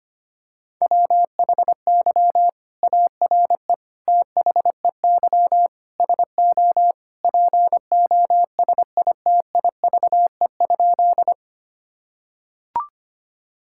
Morse code, heard as a keyed tone, about 25 wpm, 700 Hz. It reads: WHY ARE THEY SO POSITIVE?
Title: Sentences from Top 1000 Words - Review Part 5 - 25wpm